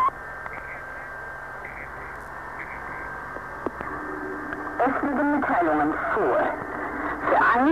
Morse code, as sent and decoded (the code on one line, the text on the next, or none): none